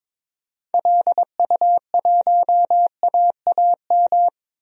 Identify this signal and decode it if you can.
Morse code, heard as a keyed tone, about 22 wpm, 700 Hz. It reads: LU1AAM
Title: DXCC-100 Call Signs - 6 Characters - 22wpm